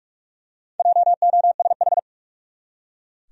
Morse code, heard as a keyed tone, about 46 wpm, 700 Hz. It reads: JOSH
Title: USA Common Names - Max 4 Letters - 45wpm